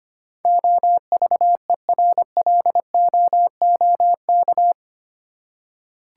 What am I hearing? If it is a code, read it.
Morse code, heard as a keyed tone, about 25 wpm, 700 Hz. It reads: OVERLOOK
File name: Eight Letter Words - 25wpm